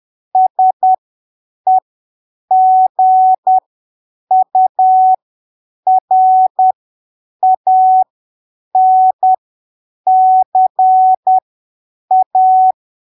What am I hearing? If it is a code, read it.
Morse code, heard as a keyed tone, about 10 wpm, spaced as for 7 wpm, 750 Hz. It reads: SEGURANCA
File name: arquivos / texto / texto 03 10ppm